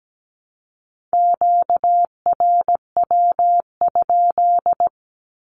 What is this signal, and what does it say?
Morse code, heard as a keyed tone, about 17 wpm, 700 Hz. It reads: QRW?
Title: Q-Codes Repeated 3x 17wpm